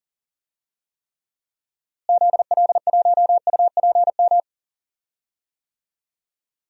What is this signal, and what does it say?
Morse code, heard as a keyed tone, about 40 wpm, 700 Hz. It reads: ZL1UPM
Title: DXCC-100 Call Signs - 6 Characters - Repeated 3x - 40wpm